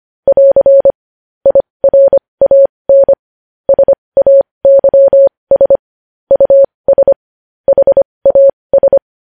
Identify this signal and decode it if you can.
Morse code, heard as a keyed tone, about 25 wpm, spaced as for 20 wpm, 550 Hz. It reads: <AR> IRAN SAYS US HAS
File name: podcast / News headlines-1x-25x20